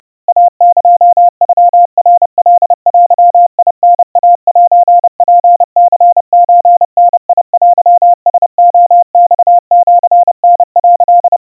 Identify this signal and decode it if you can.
Morse code (Japanese Wabun code), heard as a keyed tone, about 30 wpm, 700 Hz. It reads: イエノナカデタイセツニソダテラレマシタ」
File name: 110 cpm 700 Hz Crispy